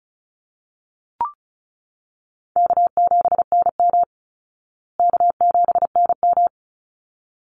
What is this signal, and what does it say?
Morse code, heard as a keyed tone, about 34 wpm, 700 Hz. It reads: K7NM K7NM
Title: Easy Call Signs Repeated 2x - 35wpm